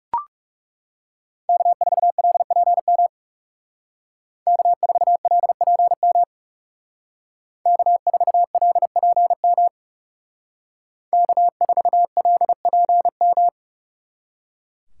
Morse code, heard as a keyed tone, about 37 wpm, 700 Hz. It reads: K4LPM K4LPM K4LPM K4LPM
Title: Speed Racing - USA 1x3 Call Signs 30wpm